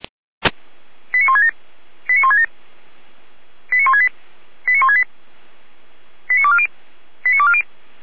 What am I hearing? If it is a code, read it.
Five-tone selective call (ZVEI), heard as quick runs of five tones, tones of about 70 ms. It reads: 89167 89167 89168 89168 89240 89240